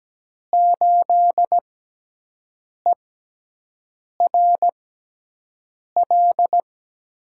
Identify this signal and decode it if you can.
Morse code, heard as a keyed tone, about 17 wpm, spaced as for 6 wpm, 700 Hz. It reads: 8ERL
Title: USA Call Signs - 5 Characters - 6x Character Spacing - 17wpm